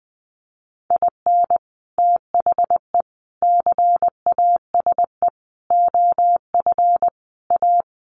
Morse code, heard as a keyed tone, about 20 wpm, 700 Hz. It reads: IN THE CASE OF A
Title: Sets of 5 Words Encore 20wpm